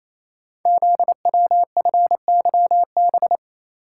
Morse code, heard as a keyed tone, about 29 wpm, 700 Hz. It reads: ZWFYB
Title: Code Groups - 5 Letters - 28wpm